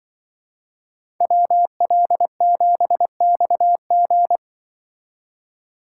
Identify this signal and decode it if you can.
Morse code, heard as a keyed tone, about 25 wpm, 700 Hz. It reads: WL7XG
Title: Speed Racing - USA Call Signs 20wpm